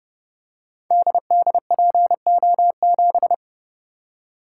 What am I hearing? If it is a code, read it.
Morse code, heard as a keyed tone, about 30 wpm, 700 Hz. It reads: DDPO7
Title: Code Groups - 5 Letter:Numbers - 30wpm